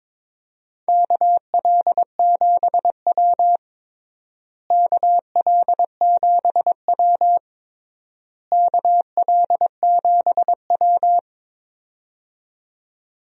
Morse code, heard as a keyed tone, about 22 wpm, 700 Hz. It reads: KL7W KL7W KL7W
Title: DXCC-100 Call Signs - 4 Characters - Repeated 3x - 22wpm